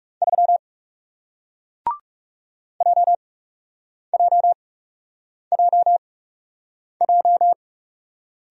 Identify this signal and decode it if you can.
Morse code, heard as a keyed tone, about 40 wpm, spaced as for 8 wpm, 700 Hz. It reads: 3 JJJJ